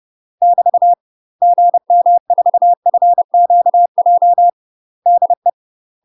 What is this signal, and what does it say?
Morse code, heard as a keyed tone, about 30 wpm, 700 Hz. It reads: <BT> GM4FQJ DE
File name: UK MASTER QSOs TRAINING / 30WPM / UK MASTER QSOs 0018